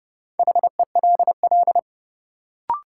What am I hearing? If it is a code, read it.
Morse code, heard as a keyed tone, about 30 wpm, 700 Hz. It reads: HELL
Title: Sets of 5 Words Encore - New Word Drill - 30wpm